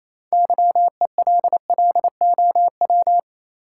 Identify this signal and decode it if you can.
Morse code, heard as a keyed tone, about 28 wpm, 700 Hz. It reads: YELLOW
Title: Six Letter Words - 28wpm